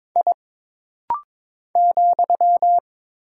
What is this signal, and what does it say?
Morse code, heard as a keyed tone, about 22 wpm, 700 Hz. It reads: I ,
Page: This rapid-fire timing is geared to push your ICR (Instant Character Recognition) skills.